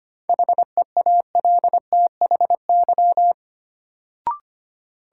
Morse code, heard as a keyed tone, about 25 wpm, 700 Hz. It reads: HEALTHY
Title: Sentences from Top 1000 Words - Review Part 5 - New Word Drill - 25wpm